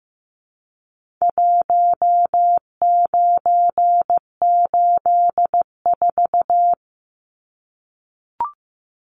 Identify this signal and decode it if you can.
Morse code, heard as a keyed tone, about 15 wpm, 700 Hz. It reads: 1984
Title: Numbers - Contest Style - 4-Digit 15wpm